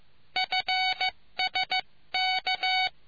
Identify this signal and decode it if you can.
Morse code, heard as a keyed tone, about 15 wpm, 750 Hz. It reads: FSK